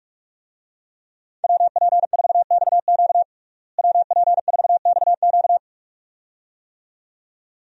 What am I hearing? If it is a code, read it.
Morse code, heard as a keyed tone, about 42 wpm, 700 Hz. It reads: WP4XQ WP4XQ
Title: USA Call Signs - 5 Characters - Repeated 2x - 45wpm